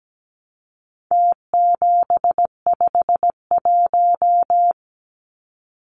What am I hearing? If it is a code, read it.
Morse code, heard as a keyed tone, about 17 wpm, 700 Hz. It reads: T751
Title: Numbers - Contest Style - 4-Digit 17wpm